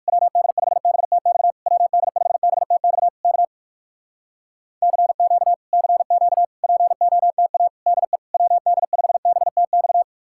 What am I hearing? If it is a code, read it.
Morse code, heard as a keyed tone, about 67 wpm, spaced as for 57 wpm, 700 Hz. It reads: WD5BTX WD5BTX K CQ CQ POTA DE WD5BTX